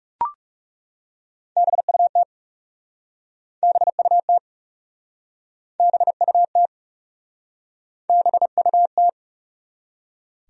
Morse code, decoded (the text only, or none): BUT BUT BUT BUT